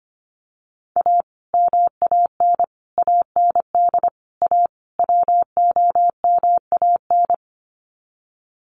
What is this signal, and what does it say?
Morse code, heard as a keyed tone, about 25 wpm, 700 Hz. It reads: A MAN AND A WOMAN
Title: Sets of 5 Words Encore 25wpm